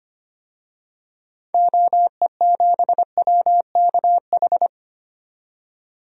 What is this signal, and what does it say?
Morse code, heard as a keyed tone, about 25 wpm, 700 Hz. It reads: OE7WKH
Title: DXCC-100 Call Signs - 6 Characters - 25wpm